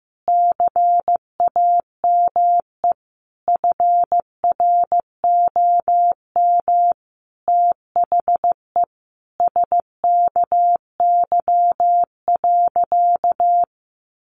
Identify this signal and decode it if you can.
Morse code, heard as a keyed tone, about 15 wpm, 700 Hz. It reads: CAME FROM THE SKY.
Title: Sentences from Top 1200 Words 15wpm